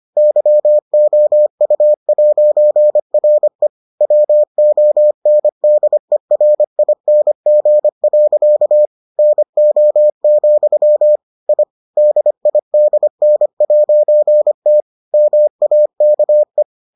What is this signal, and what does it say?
Morse code, heard as a keyed tone, about 25 wpm, 600 Hz. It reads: YOU'RE WONDERING. NO, I DIDN'T MAKE